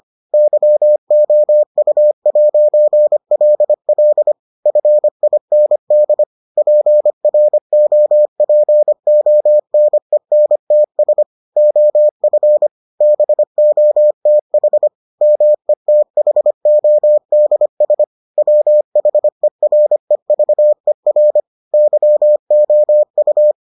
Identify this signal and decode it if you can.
Morse code, heard as a keyed tone, about 25 wpm, 600 Hz. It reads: YOU'LL FIND PROPONENTS OF BOTH METHODS WHEREVER YOU